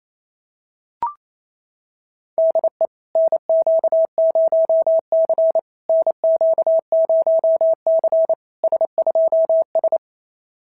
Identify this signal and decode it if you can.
Morse code, heard as a keyed tone, about 31 wpm, 650 Hz. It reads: DE NQ0C NQ0C S2S
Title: SOTA QSO Practice 28wpm